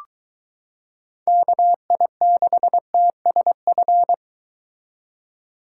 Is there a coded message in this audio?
Yes: Morse code from 1.3 to 4.1 s, heard as a keyed tone, about 22 wpm, 700 Hz. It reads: KI6TSF